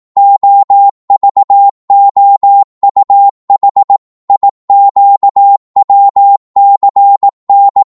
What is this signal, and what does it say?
Morse code, heard as a keyed tone, about 18 wpm, 800 Hz. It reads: OVOUH IQWCN
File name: Morse / ltr-18wpm